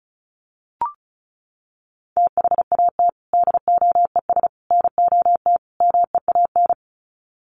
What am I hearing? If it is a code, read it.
Morse code, heard as a keyed tone, about 34 wpm, 700 Hz. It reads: THAT DOES NOT MEAN
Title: Sets of 4 Words 35wpm